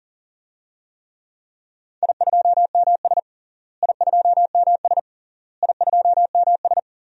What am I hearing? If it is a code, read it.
Morse code, heard as a keyed tone, about 40 wpm, 700 Hz. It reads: I2MS I2MS I2MS
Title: DXCC-100 Call Signs - 4 Characters - Repeated 3x - 40wpm